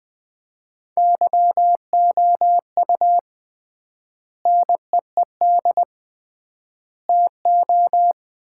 Morse code, heard as a keyed tone, about 20 wpm, 700 Hz. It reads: YOU NEED TO
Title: 3x Word-Spacing - Sets of 3 Words - 20wpm